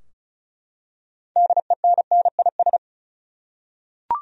Morse code, heard as a keyed tone, about 36 wpm, 700 Hz. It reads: DENNIS
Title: USA Common Names - 35wpm